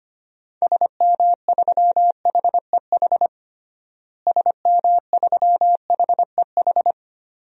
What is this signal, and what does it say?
Morse code, heard as a keyed tone, about 25 wpm, 700 Hz. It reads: SM3HEH SM3HEH
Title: DXCC-100 Call Signs - 6 Characters - Repeated 2x - 25wpm